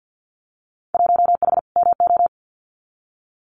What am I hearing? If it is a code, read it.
Morse code, heard as a keyed tone, about 48 wpm, 700 Hz. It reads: 1H MO
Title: Winter Field Day Exchanges 50wpm